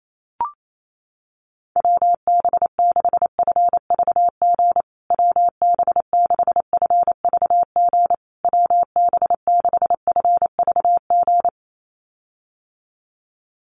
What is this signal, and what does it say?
Morse code, heard as a keyed tone, about 28 wpm, 700 Hz. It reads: WB6FVG WB6FVG WB6FVG
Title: USA Call Signs Repeated 3x 28wpm